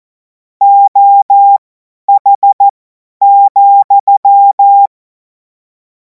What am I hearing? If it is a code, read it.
Morse code, heard as a keyed tone, about 14 wpm, spaced as for 10 wpm, 800 Hz. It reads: OH,
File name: Morsekurs / Witze / Witz 12. Mai 2026 50 F70